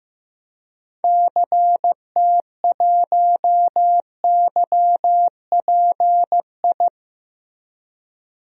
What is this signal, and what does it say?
Morse code, heard as a keyed tone, about 15 wpm, 700 Hz. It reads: CT1YPI
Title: DXCC-100 Call Signs - 6 Characters - Repeated 3x - 15wpm